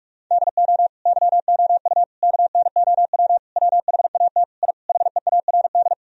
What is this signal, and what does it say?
Morse code, heard as a keyed tone, about 46 wpm, 700 Hz. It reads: DO YOU KNOW WHAT I HEARD
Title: Sentences from Top 1000 Words - Review Part 4 - 45wpm